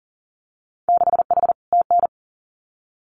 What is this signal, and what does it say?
Morse code, heard as a keyed tone, about 43 wpm, 700 Hz. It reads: 6H TN
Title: Winter Field Day Exchanges Encore 40wpm